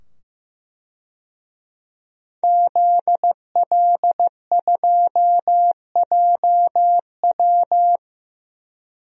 Morse code, heard as a keyed tone, about 15 wpm, 700 Hz. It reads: ZL2JW
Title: DXCC-100 Call Signs - 5 Characters - 15wpm